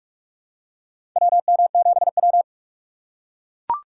Morse code, heard as a keyed tone, about 43 wpm, 700 Hz. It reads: WM7W
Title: USA Call Signs - 4 Characters - 3x Character Spacing - 45wpm